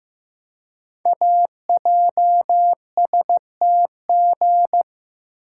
Morse code, heard as a keyed tone, about 15 wpm, 700 Hz. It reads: AJSTG
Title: Code Groups - 5 Letters - 15wpm